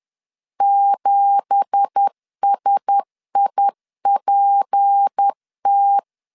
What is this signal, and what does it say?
Morse code, heard as a keyed tone, about 10 wpm, 800 Hz. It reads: 7SIPT
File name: LEZIONI / 9 2867 10wpm 2733